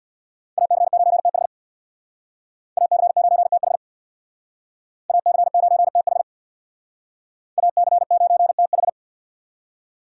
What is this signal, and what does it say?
Morse code, heard as a keyed tone, about 64 wpm, 700 Hz. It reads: AC9TH AC9TH AC9TH AC9TH